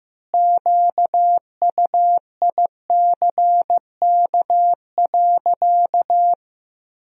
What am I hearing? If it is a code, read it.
Morse code, heard as a keyed tone, about 15 wpm, 700 Hz. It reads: QUICK.
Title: 4x Word-Spacing - Sentences from Top 500 Words - 15wpm